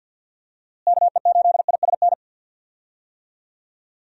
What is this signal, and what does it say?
Morse code, heard as a keyed tone, about 52 wpm, 700 Hz. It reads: KE8ISN
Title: USA Call Signs - 6 Characters - Repeated 2x - 50wpm